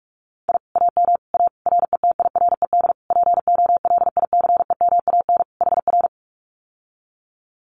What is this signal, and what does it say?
Morse code, heard as a keyed tone, about 44 wpm, 700 Hz. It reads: I AM A RETIRED POLICEMAN HR
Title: Common QSO Phrases Encore 45wpm